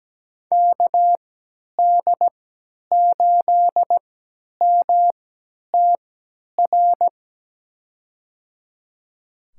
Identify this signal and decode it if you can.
Morse code, heard as a keyed tone, about 17 wpm, spaced as for 10 wpm, 700 Hz. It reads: KD8MTR
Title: USA Call Signs - 6 Characters - 3x Character Spacing - 17wpm